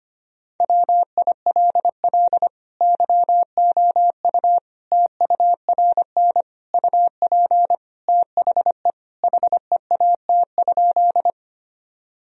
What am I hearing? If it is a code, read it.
Morse code, heard as a keyed tone, about 25 wpm, 700 Hz. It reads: WILL YOU TURN UP THE HEAT?
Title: Sentences from Top 1000 Words - Review Part 3 - 25wpm